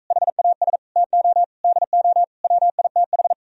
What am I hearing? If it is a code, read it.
Morse code, heard as a keyed tone, about 45 wpm, 700 Hz. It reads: HAS TO DO WITH